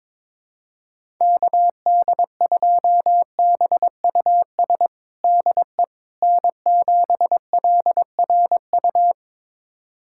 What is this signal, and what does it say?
Morse code, heard as a keyed tone, about 22 wpm, 700 Hz. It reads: KD2BUS DE N7LRU
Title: USA Callsign DE Callsign 22wpm